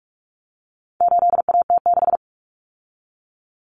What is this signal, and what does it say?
Morse code, heard as a keyed tone, about 44 wpm, 700 Hz. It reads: 8AT6
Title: Common Cut Numbers - 4-Digit 45wpm